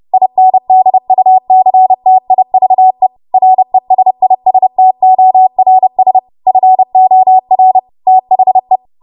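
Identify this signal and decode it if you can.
Morse code, heard as a keyed tone, about 30 wpm, 750 Hz. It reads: INDUCTIVE RESISTORS FOR THE
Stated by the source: W1AW Web Code Practice Files, 30 WPM run from ARRL; text from April 2023 QST.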